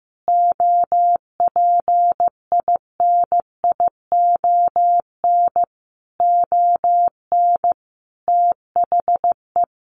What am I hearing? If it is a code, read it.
Morse code, heard as a keyed tone, about 15 wpm, 700 Hz. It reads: OPINION ON THE
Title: Sentences from Top 1200 Words 15wpm